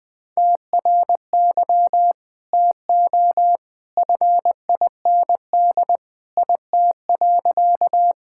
Morse code, heard as a keyed tone, about 21 wpm, 700 Hz. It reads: TRY TO FIND IT.
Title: Sentences with 2-Word Phrases - 20wpm